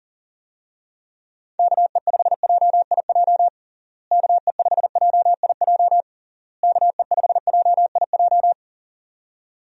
Morse code, heard as a keyed tone, about 40 wpm, 700 Hz. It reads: KE5JIJ KE5JIJ KE5JIJ
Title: USA Call Signs - 6 Characters - Repeated 3x - 40wpm